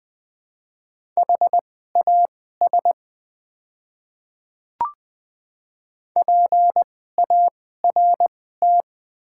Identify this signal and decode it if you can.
Morse code, heard as a keyed tone, about 20 wpm, spaced as for 14 wpm, 700 Hz. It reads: HAS PART
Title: 2x Character Spacing - Top 500 Words - 20wpm